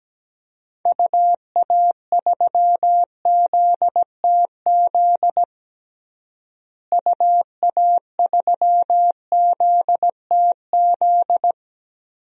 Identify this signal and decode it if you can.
Morse code, heard as a keyed tone, about 17 wpm, 700 Hz. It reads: UA3ZTZ UA3ZTZ
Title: DXCC-100 Call Signs - 6 Characters - Repeated 2x - 17wpm